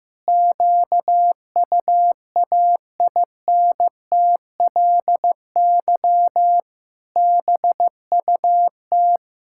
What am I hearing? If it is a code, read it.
Morse code, heard as a keyed tone, about 16 wpm, 700 Hz. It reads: QUAINTLY BUT
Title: Pangram Challenge v2 - 15wpm